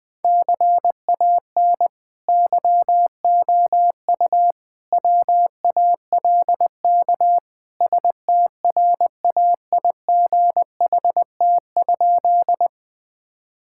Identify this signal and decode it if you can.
Morse code, heard as a keyed tone, about 20 wpm, 700 Hz. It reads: CAN YOU WALK STRAIGHT?
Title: Sentences from Top 1000 Words - Review Part 6 - 20wpm